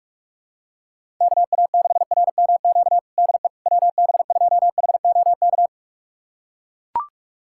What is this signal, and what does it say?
Morse code, heard as a keyed tone, about 41 wpm, 700 Hz. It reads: KA6RMQ DE WB2HOK
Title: USA Callsign DE Callsign - 6 Characters - 45wpm